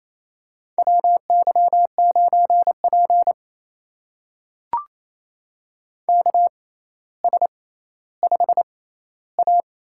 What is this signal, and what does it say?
Morse code, heard as a keyed tone, about 28 wpm, spaced as for 14 wpm, 700 Hz. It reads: WY9P KS5A
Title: USA Call Signs - 4 Characters - 6x Character Spacing - 28wpm